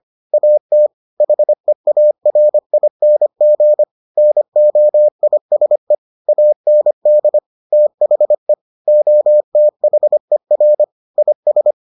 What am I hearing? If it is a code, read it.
Morse code, heard as a keyed tone, about 25 wpm, 600 Hz. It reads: AT HEARING NOISE AND THE OTHER IS